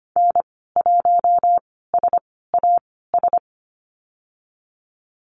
Morse code, heard as a keyed tone, about 25 wpm, spaced as for 16 wpm, 700 Hz. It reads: N1SAS
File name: Lesson 15g - Number 5 - Call Signs 25wpm